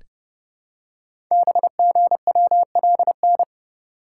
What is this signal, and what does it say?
Morse code, heard as a keyed tone, about 30 wpm, 700 Hz. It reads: BGWLN
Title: Code Groups - 5 Letters - 30wpm